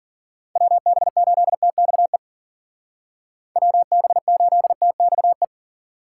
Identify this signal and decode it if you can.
Morse code, heard as a keyed tone, about 49 wpm, 700 Hz. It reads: WB8TXE WB8TXE
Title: Speed Racing - USA 2x3 Call Signs 40wpm